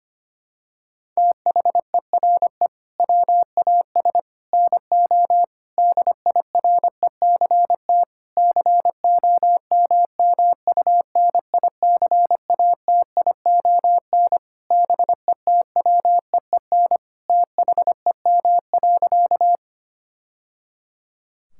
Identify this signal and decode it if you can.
Morse code, heard as a keyed tone, about 25 wpm, 700 Hz. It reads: THERE WAS NO DIRECT COMMUNICATION BETWEEN THEM.
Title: Sentences from Top 1000 Words - Review Part 6 - 25wpm